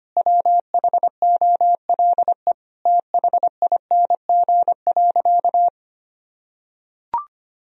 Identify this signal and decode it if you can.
Morse code, heard as a keyed tone, about 25 wpm, 700 Hz. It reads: WHOLE THING.
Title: Sentences from Top 1000 Words - Review Part 1 - 25wpm